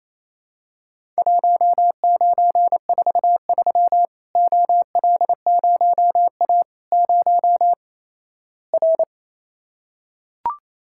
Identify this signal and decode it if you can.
Morse code, heard as a keyed tone, about 28 wpm, 700 Hz. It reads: 1943 OL0A 0 R
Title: Worked All Europe DX Contest - QTC Exchange - 28wpm